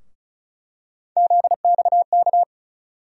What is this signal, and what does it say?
Morse code, heard as a keyed tone, about 36 wpm, 700 Hz. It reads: ZXK
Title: USA Call Sign Postfixes - 3 Letters 35wpm